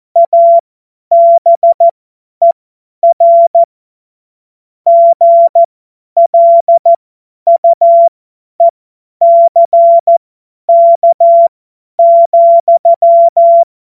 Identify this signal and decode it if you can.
Morse code, heard as a keyed tone, about 14 wpm, spaced as for 10 wpm, 680 Hz. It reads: ABER GLUECK,
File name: Morsekurs / Witze / Witz 02 Januar 2024 50 F70